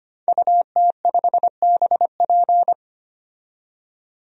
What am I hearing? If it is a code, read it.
Morse code, heard as a keyed tone, about 25 wpm, 700 Hz. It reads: UT5BP
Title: DXCC-100 Call Signs - 5 Characters - 25wpm